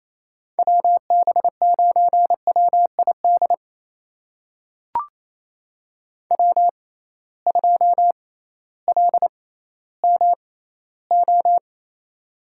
Morse code, heard as a keyed tone, about 28 wpm, spaced as for 15 wpm, 700 Hz. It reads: WB9WID W2LMO
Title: USA Call Signs - 6x Character Spacing - 28wpm